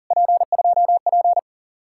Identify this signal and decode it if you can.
Morse code, heard as a keyed tone, about 52 wpm, 700 Hz. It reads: P2P